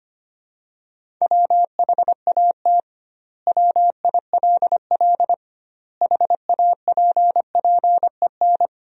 Morse code, heard as a keyed tone, about 25 wpm, 700 Hz. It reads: WHAT WILL HAPPEN
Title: 2x Word-Spacing - Sentences from Top 400 Words - 25wpm